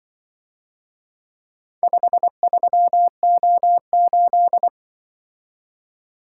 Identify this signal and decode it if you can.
Morse code, heard as a keyed tone, about 24 wpm, 700 Hz. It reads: 53O8
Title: Speed-Racing - Four Letter-Number Combinations - 20wpm